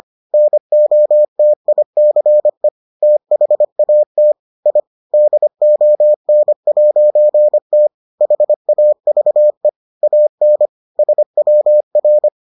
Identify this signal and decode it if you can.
Morse code, heard as a keyed tone, about 25 wpm, 600 Hz. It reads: NOTICE THAT I DON'T HAVE AN SWR